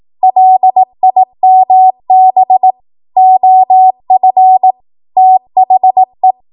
Morse code, heard as a keyed tone, about 18 wpm, 750 Hz. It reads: LIMB OF THE
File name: Morse / 18WPM